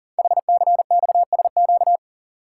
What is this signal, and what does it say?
Morse code, heard as a keyed tone, about 42 wpm, 700 Hz. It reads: HCXSQ
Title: Code Groups - 5 Letters - 40wpm